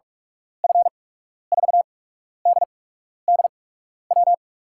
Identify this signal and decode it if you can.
Morse code, heard as a keyed tone, about 46 wpm, spaced as for 12 wpm, 700 Hz. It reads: F4DDW
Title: USA Call Signs - 6 Characters - 8x Character Spacing - 45wpm